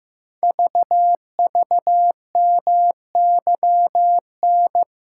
Morse code, heard as a keyed tone, about 15 wpm, 700 Hz. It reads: VVMYN
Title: Code Groups - 5 Letters - 15wpm